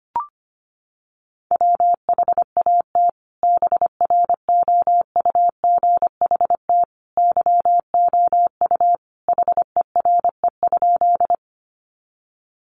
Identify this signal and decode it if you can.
Morse code, heard as a keyed tone, about 25 wpm, 700 Hz. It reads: WHAT BROUGHT YOU HERE?
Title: Sentences from Top 500 Words 25wpm